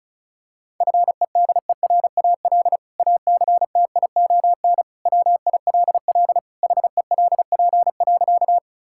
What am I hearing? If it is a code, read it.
Morse code, heard as a keyed tone, about 36 wpm, 700 Hz. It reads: FEDERAL ACTION WILL HELP.
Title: Sentences from Top 1000 Words - Review Part 5 - 35wpm